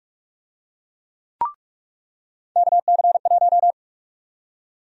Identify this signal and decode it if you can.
Morse code, heard as a keyed tone, about 45 wpm, 700 Hz. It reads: KC1
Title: USA Call Sign Prefixes - 45wpm